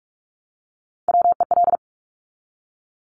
Morse code, heard as a keyed tone, about 46 wpm, 700 Hz. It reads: WEL
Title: Top Three Letter Combinations in Words 45wpm